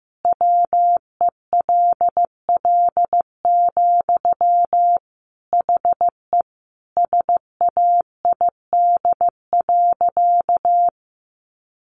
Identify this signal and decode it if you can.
Morse code, heard as a keyed tone, about 15 wpm, 700 Hz. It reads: WELL, HE SAID.
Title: Sentences from Top 700 Words 15wpm